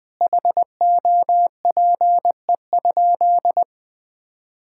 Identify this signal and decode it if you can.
Morse code, heard as a keyed tone, about 20 wpm, 700 Hz. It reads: HOPE?
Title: Sentences from Top 1000 Words - Review Part 5 - 20wpm